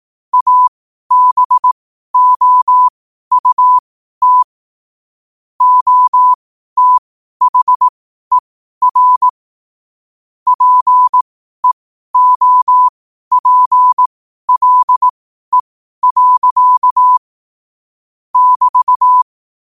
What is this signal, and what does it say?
Morse code, heard as a keyed tone, about 18 wpm, spaced as for 12 wpm, 1000 Hz. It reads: ABOUT OTHER PEOPLE. <BT>
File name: qotdfiles / 12wpm